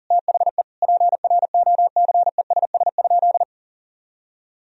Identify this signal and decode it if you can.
Morse code, heard as a keyed tone, about 40 wpm, 700 Hz. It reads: THE PROCESS?
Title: Sentences from Top 1000 Words - Review Part 3 - 40wpm